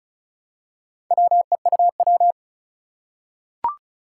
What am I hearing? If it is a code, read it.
Morse code, heard as a keyed tone, about 36 wpm, 700 Hz. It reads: WEUW
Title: Four Challenging Letter-Number Combinations 35wpm